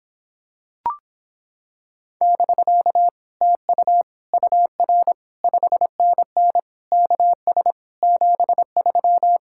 Morse code, heard as a keyed tone, about 26 wpm, 700 Hz. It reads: <BK> TU UR 5NN KS 73